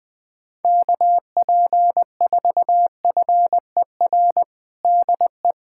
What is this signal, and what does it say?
Morse code, heard as a keyed tone, about 20 wpm, 700 Hz. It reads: KP4FER DE
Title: USA Callsign DE Callsign - 6 Characters - 20wpm